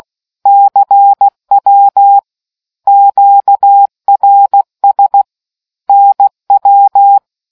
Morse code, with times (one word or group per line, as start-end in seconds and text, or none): 0.5-2.2 CW
2.9-5.2 QRS
5.9-7.2 NW